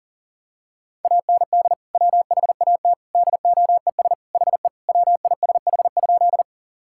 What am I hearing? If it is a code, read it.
Morse code, heard as a keyed tone, about 37 wpm, 700 Hz. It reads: AND WHAT DOES HE WISH?
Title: Sentences from Top 1600 Words 40wpm